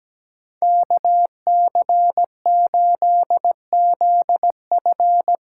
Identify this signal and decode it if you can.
Morse code, heard as a keyed tone, about 19 wpm, 700 Hz. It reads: KC8ZF
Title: Speed Racing - USA Call Signs 17wpm